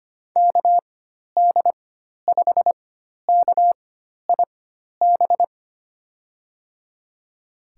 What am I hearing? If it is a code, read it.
Morse code, heard as a keyed tone, about 25 wpm, spaced as for 12 wpm, 700 Hz. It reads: KD5KIB
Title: USA Call Signs - 6 Characters - 4x Character Spacing - 25wpm